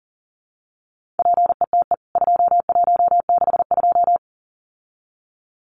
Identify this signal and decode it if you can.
Morse code, heard as a keyed tone, about 40 wpm, 700 Hz. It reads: PETE 2162
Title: CWT Exchange 40wpm